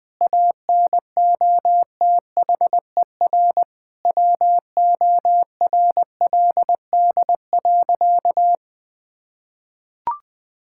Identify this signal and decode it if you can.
Morse code, heard as a keyed tone, about 20 wpm, 700 Hz. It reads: ANOTHER WORLD.
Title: Difficult Sentences with Hints - 20wpm